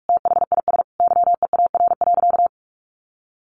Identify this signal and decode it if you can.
Morse code, heard as a keyed tone, about 44 wpm, 700 Hz. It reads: THIS YEAR.
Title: Sentences from Top 1400 Words 45wpm